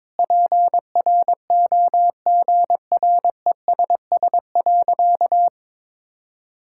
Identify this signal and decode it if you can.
Morse code, heard as a keyed tone, about 21 wpm, 700 Hz. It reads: PROGRESS.
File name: Sentences with 7-Word Phrases - 22wpm